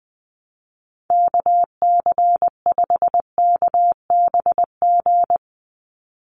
Morse code, heard as a keyed tone, about 20 wpm, 700 Hz. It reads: KC5KBG
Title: USA Call Signs 20wpm